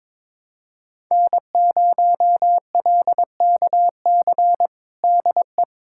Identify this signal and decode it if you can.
Morse code, heard as a keyed tone, about 22 wpm, 700 Hz. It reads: N0LKC DE
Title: USA Callsign DE Callsign - 5 Characters - 22wpm